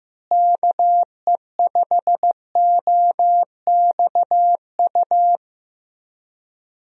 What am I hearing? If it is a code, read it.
Morse code, heard as a keyed tone, about 15 wpm, 700 Hz. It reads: KE5OXU
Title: USA Call Signs - 6 Characters - Repeated 3x - 15wpm